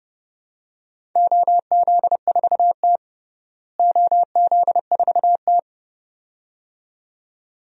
Morse code, heard as a keyed tone, about 30 wpm, 700 Hz. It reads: OZ4T OZ4T
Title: DXCC-100 Call Signs - 4 Characters - Repeated 2x - 30wpm